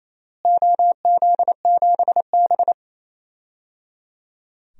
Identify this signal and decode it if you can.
Morse code, heard as a keyed tone, about 28 wpm, 700 Hz. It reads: OZ7B
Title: DXCC-100 Call Signs - 4 Characters - Repeated 3x - 28wpm